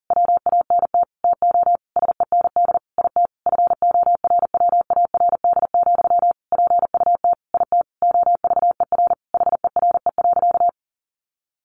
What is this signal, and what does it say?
Morse code, heard as a keyed tone, about 40 wpm, 700 Hz. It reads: WANT TO SEND IT FORWARD, PUT IT OVER HERE.
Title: Sentences from Top 1500 Words 40wpm